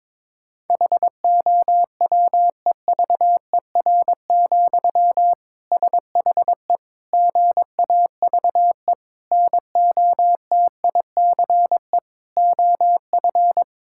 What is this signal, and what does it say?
Morse code, heard as a keyed tone, about 22 wpm, 700 Hz. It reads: HOWEVER, SHE GAVE NOTICE OF
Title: Sentences from Top 1000 Words - Review Part 6 - 22wpm